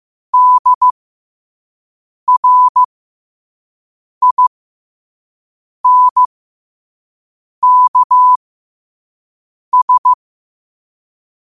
Morse code, heard as a keyed tone, about 15 wpm, spaced as for 5 wpm, 1000 Hz. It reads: DRINKS
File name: qotdfiles / 5wpm